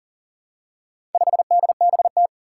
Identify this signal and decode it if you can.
Morse code, heard as a keyed tone, about 40 wpm, 700 Hz. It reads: 5DBT
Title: Four Challenging Letter-Number Combinations 40wpm